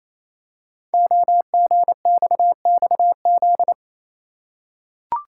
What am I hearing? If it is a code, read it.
Morse code, heard as a keyed tone, about 27 wpm, 700 Hz. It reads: OGXXZ
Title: Code Groups - 5 Letter:Numbers - 28wpm